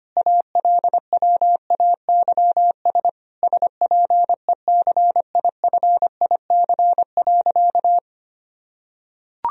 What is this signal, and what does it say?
Morse code, heard as a keyed tone, about 25 wpm, 700 Hz. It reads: ALWAYS SPECIFIC.
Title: Sentences from Top 1000 Words - Review Part 3 - 25wpm